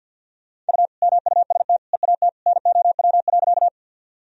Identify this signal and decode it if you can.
Morse code, heard as a keyed tone, about 46 wpm, 700 Hz. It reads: U MUST EAT NOW.
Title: Sentences from Top 300 Words and Abbreviations 50wpm